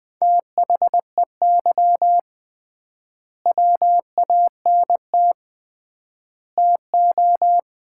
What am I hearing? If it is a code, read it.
Morse code, heard as a keyed tone, about 20 wpm, 700 Hz. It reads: THEY WANT TO
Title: 3x Word-Spacing - Sets of 3 Words - 20wpm